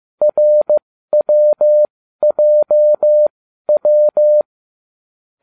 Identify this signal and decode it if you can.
Morse code, heard as a keyed tone, about 15 wpm, spaced as for 12 wpm, 600 Hz. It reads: RWJW